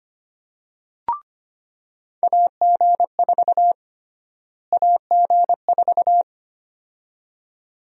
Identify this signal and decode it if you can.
Morse code, heard as a keyed tone, about 25 wpm, 700 Hz. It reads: AG4 AG4
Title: USA Call Sign Prefixes - Repeated 2x - 25wpm